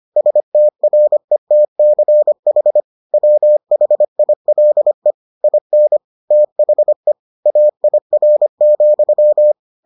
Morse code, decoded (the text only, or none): STRETCH WHILE IN THE AIR,